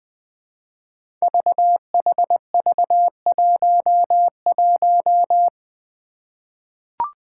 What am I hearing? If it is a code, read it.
Morse code, heard as a keyed tone, about 20 wpm, 700 Hz. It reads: VHV11
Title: Five Challenging Letter-Number Combinations 20wpm